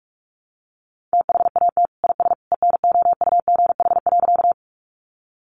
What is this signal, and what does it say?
Morse code, heard as a keyed tone, about 44 wpm, 700 Hz. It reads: THAT IS ENOUGH.
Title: Sentences from Top 400 Words 45wpm